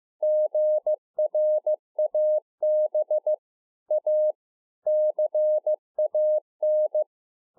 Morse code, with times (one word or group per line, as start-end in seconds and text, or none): 0.2-3.3 GRAB
3.9-4.3 A
4.9-7.0 CAN